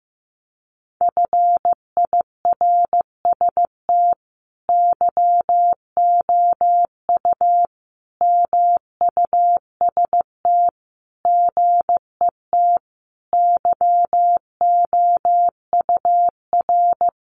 Morse code, heard as a keyed tone, about 15 wpm, 700 Hz. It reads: FIRST YOU MUST GET YOUR
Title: Sentences from Top 1500 Words 15wpm